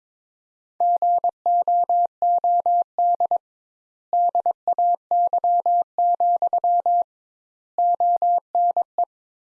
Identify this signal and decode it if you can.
Morse code, heard as a keyed tone, about 21 wpm, 700 Hz. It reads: GOOD DAY, ONE
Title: Sentences Top 200 Words - 22wpm